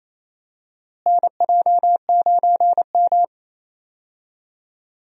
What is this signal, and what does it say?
Morse code, heard as a keyed tone, about 28 wpm, 700 Hz. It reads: NJ9M
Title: USA Call Signs - 4 Characters - Repeated 3x - 28wpm